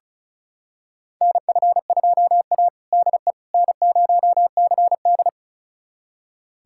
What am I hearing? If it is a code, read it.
Morse code, heard as a keyed tone, about 34 wpm, 700 Hz. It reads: NF2A DE N0CD
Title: USA Callsign DE Callsign - 4 Characters - 35wpm